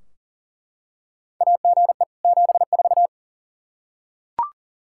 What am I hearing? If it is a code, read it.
Morse code, heard as a keyed tone, about 40 wpm, 700 Hz. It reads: AGE 74
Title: Practice QSO - Exchange 3 - More Info - 40wpm